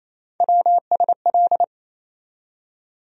Morse code, heard as a keyed tone, about 28 wpm, 700 Hz. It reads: WSL
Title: USA Call Sign Postfixes - 3 Letters 28wpm